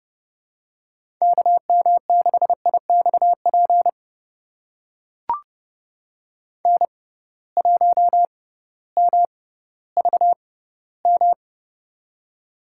Morse code, heard as a keyed tone, about 30 wpm, spaced as for 16 wpm, 700 Hz. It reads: KM6IXP N1MVM
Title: USA Call Signs - 6x Character Spacing - 30wpm